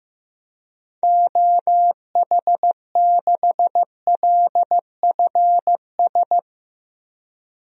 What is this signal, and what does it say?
Morse code, heard as a keyed tone, about 15 wpm, 700 Hz. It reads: OH6LFS
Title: DXCC-100 Call Signs - 6 Characters - 15wpm